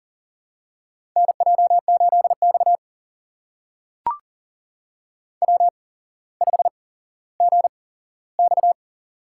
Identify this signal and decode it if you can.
Morse code, heard as a keyed tone, about 40 wpm, spaced as for 17 wpm, 700 Hz. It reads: NJ8X W5GX
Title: USA Call Signs - 4 Characters - 8x Character Spacing - 40wpm